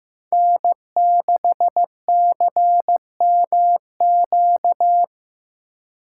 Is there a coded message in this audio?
Yes: Morse code from 0.3 to 5.0 s, heard as a keyed tone, about 19 wpm, 700 Hz. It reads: N6CMQ